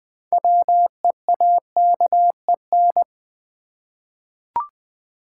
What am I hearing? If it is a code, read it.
Morse code, heard as a keyed tone, about 20 wpm, 700 Hz. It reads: WEAKEN
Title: 6 Letter Words - First 3 Letters Given - 20wpm